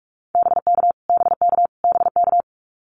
Morse code, heard as a keyed tone, about 44 wpm, 700 Hz. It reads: BK BK BK
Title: Abbreviations Repeated 3x - 45wpm